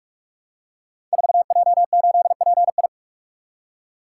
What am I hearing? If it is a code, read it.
Morse code, heard as a keyed tone, about 44 wpm, 700 Hz. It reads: 4J8PI
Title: Code Groups - 5 Letter:Numbers - 45wpm